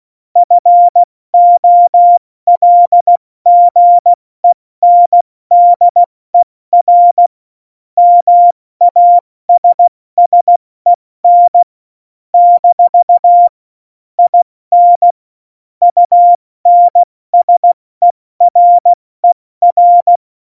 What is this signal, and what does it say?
Morse code, heard as a keyed tone, about 16 wpm, spaced as for 14 wpm, 700 Hz. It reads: FOLGENDERMASSEN - IN UNSERER
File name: Morsekurs / Witze / Witz 19. Februar 2019, 70 F80